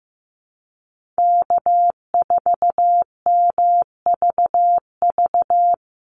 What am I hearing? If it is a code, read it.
Morse code, heard as a keyed tone, about 15 wpm, 700 Hz. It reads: K4MVV
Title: USA Call Signs Repeated 3x 15wpm